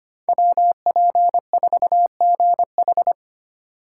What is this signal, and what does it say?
Morse code, heard as a keyed tone, about 25 wpm, 700 Hz. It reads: WP4GH
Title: USA Call Signs - 5 Characters - 2x Character Spacing - 25wpm